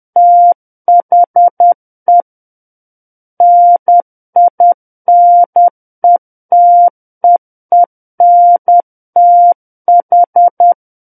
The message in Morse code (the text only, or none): THE NINETEENTH